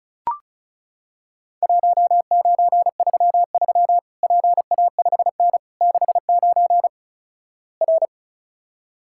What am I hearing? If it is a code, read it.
Morse code, heard as a keyed tone, about 36 wpm, 700 Hz. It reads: 1933 PA5N 69 R
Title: Worked All Europe DX Contest - QTC Exchange - 35wpm